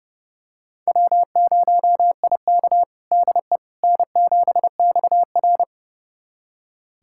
Morse code, heard as a keyed tone, about 30 wpm, 700 Hz. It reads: W0IK DE N7XR
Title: USA Callsign DE Callsign - 4 Characters - 30wpm